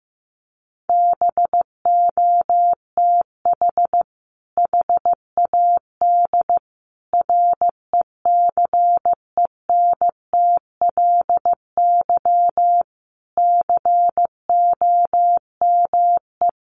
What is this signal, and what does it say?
Morse code, heard as a keyed tone, about 15 wpm, 700 Hz. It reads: BOTH HAD RECENTLY COME
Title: Sentences from Top 1200 Words 15wpm